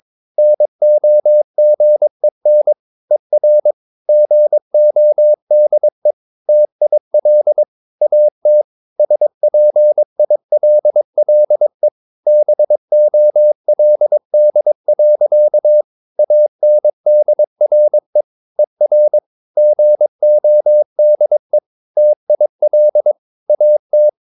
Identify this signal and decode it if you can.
Morse code, heard as a keyed tone, about 22 wpm, 600 Hz. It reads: NOGEN ER GODE TIL AT SPILLE BOLD. ANDRE ER GODE TIL AT
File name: Ferie QRS 22wpm